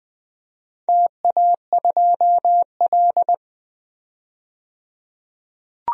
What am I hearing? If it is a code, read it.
Morse code, heard as a keyed tone, about 21 wpm, 700 Hz. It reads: TA2L
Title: DXCC-100 Call Signs - 4 Characters - 20wpm